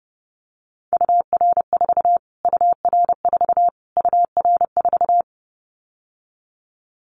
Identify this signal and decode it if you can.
Morse code, heard as a keyed tone, about 30 wpm, 700 Hz. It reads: UR4 UR4 UR4
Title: Callsign Prefixes for Top 25 DXCC - Repeated 3x - 30wpm